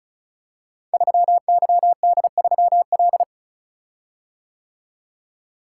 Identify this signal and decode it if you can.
Morse code, heard as a keyed tone, about 36 wpm, 700 Hz. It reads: 3YD3L
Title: Code Groups - 5 Letter:Numbers - 35wpm